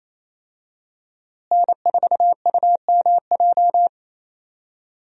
Morse code, heard as a keyed tone, about 28 wpm, 700 Hz. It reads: N4UMJ
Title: USA Call Signs - 5 Characters - Repeated 2x - 28wpm